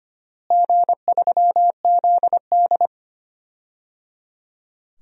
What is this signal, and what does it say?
Morse code, heard as a keyed tone, about 25 wpm, 700 Hz. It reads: G3ZD
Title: DXCC-100 Call Signs - 4 Characters - Repeated 3x - 25wpm